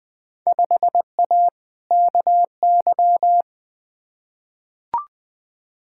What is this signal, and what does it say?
Morse code, heard as a keyed tone, about 20 wpm, 700 Hz. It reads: 5A KY
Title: Field Day Exchanges - 20wpm